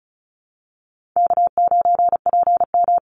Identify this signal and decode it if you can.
Morse code, heard as a keyed tone, about 34 wpm, 700 Hz. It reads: K9PM
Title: NA SOTA Activator Callsigns 35wpm